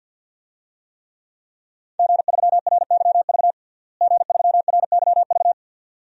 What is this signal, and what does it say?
Morse code, heard as a keyed tone, about 51 wpm, 700 Hz. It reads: G3RYV G3RYV
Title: DXCC-100 Call Signs - 5 Characters - Repeated 2x - 50wpm